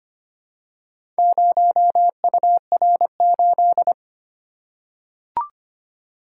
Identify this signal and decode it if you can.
Morse code, heard as a keyed tone, about 25 wpm, 700 Hz. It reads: 0UR8
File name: practice / 25wpm / Four Letter-Number Combinations - 25wpm